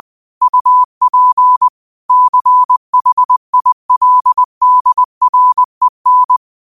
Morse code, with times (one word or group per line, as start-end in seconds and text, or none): 0.4-1.7 UP
2.1-6.4 CHILDREN